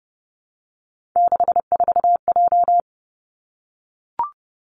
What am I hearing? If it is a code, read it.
Morse code, heard as a keyed tone, about 30 wpm, 700 Hz. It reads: 64J
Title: Three Challenging Letter-Number Combinations 30wpm